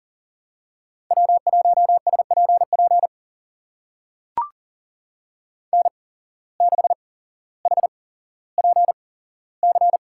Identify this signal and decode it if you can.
Morse code, heard as a keyed tone, about 42 wpm, spaced as for 17 wpm, 700 Hz. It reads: W1SPP N6HPC
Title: USA Call Signs - 5 Characters - 8x Character Spacing - 40wpm